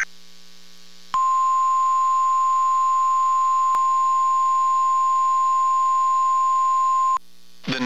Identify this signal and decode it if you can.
Emergency Alert System (SAME) tones (the 1050 Hz attention tone). It is heard from 1.1 to 7.2 s.
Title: KEC94 NOAA Weather Radio